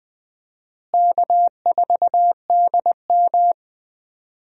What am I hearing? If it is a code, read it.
Morse code, heard as a keyed tone, about 21 wpm, 700 Hz. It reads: K4DM
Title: USA Call Signs - 4 Characters - Repeated 2x - 20wpm